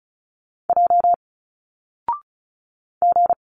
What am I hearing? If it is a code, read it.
Morse code, heard as a keyed tone, about 36 wpm, 700 Hz. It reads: J G
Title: Single Letters 35wpm